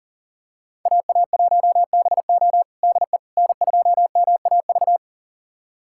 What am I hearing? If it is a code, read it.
Morse code, heard as a keyed tone, about 40 wpm, 700 Hz. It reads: AA1BO DE N2MAV
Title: USA Callsign DE Callsign - 5 Characters - 40wpm